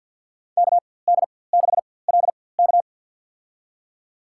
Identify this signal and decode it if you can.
Morse code, heard as a keyed tone, about 50 wpm, spaced as for 23 wpm, 700 Hz. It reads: KD6LK
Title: USA Call Signs - 5 Characters - 4x Character Spacing - 50wpm